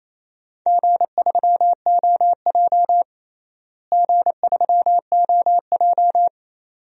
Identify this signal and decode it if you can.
Morse code, heard as a keyed tone, about 28 wpm, 700 Hz. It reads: G3OJ G3OJ
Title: DXCC-100 Call Signs - 4 Characters - Repeated 2x - 28wpm